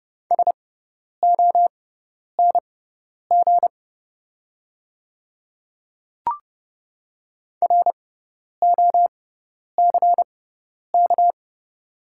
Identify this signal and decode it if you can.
Morse code, heard as a keyed tone, about 31 wpm, spaced as for 10 wpm, 700 Hz. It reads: SONG ROCK
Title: 6x Character Spacing - Top 500 Words - 30wpm